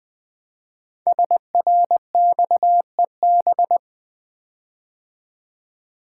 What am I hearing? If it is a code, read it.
Morse code, heard as a keyed tone, about 20 wpm, 700 Hz. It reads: SRXEB
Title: Code Groups - 5 Letters - 20wpm